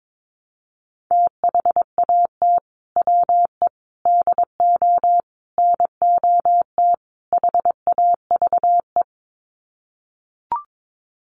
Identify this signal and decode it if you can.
Morse code, heard as a keyed tone, about 22 wpm, 700 Hz. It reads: THAT WE DO NOT HAVE
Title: Sets of 5 Words Encore 22wpm